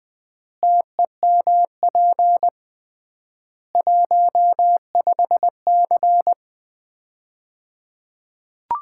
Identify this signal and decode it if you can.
Morse code, heard as a keyed tone, about 20 wpm, 700 Hz. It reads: TEMP 15C
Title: Common QSO Phrases - 3x Word-Spacing - 20wpm